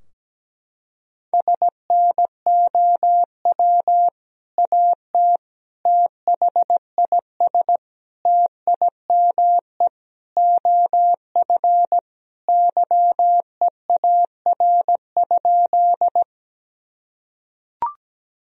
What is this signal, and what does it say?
Morse code, heard as a keyed tone, about 17 wpm, 700 Hz. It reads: SNOW AT THIS TIME OF YEAR?
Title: Sentences from Top 1000 Words - Review Part 3 - 17wpm